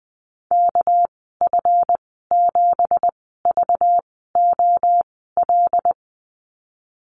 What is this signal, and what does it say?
Morse code, heard as a keyed tone, about 20 wpm, spaced as for 15 wpm, 700 Hz. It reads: KF7VOL
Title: 213 Lesson 34g - Number 8 - Call Signs 20wpm